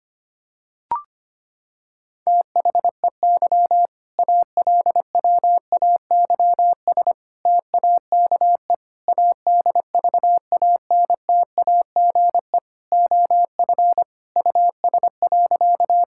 Morse code, heard as a keyed tone, about 25 wpm, 700 Hz. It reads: THEY ALWAYS TAKE ADVANTAGE OF US.
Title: Sentences from Top 1000 Words - Review Part 6 - 25wpm